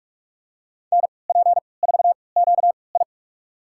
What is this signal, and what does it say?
Morse code, heard as a keyed tone, about 46 wpm, spaced as for 26 wpm, 700 Hz. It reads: NP4QI
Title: USA Call Signs - 5 Characters - 3x Character Spacing - 45wpm